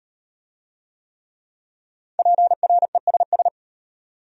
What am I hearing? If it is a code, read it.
Morse code, heard as a keyed tone, about 33 wpm, 700 Hz. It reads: PRESS